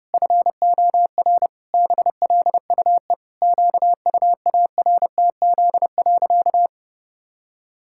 Morse code, heard as a keyed tone, about 30 wpm, 700 Hz. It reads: FOR BLUE QUARTZ.
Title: Pangram Challenge v2 - 30wpm